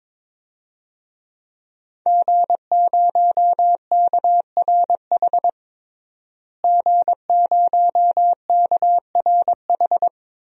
Morse code, heard as a keyed tone, about 22 wpm, 700 Hz. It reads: G0KRH G0KRH
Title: DXCC-100 Call Signs - 5 Characters - Repeated 2x - 22wpm